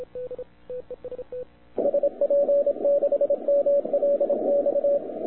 Morse code, heard as a keyed tone, about 26 wpm, 570 Hz, with radio noise. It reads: SP6MLX
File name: wwcw08 / k5zd wwcw08 0200z